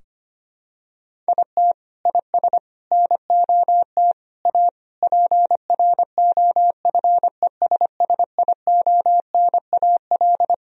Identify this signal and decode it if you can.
Morse code, heard as a keyed tone, about 25 wpm, 700 Hz. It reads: IT IS NOT A PROFESSIONAL
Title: Sentences from Top 1000 Words - Review Part 4 - 25wpm